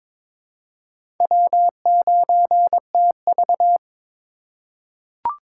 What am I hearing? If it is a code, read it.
Morse code, heard as a keyed tone, about 22 wpm, 700 Hz. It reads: W9TV
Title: USA Call Signs - 4 Characters - 2x Character Spacing - 22wpm